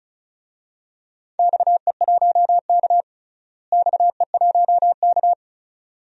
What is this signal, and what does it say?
Morse code, heard as a keyed tone, about 36 wpm, 700 Hz. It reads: XE1K XE1K
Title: DXCC-100 Call Signs - 4 Characters - Repeated 2x - 35wpm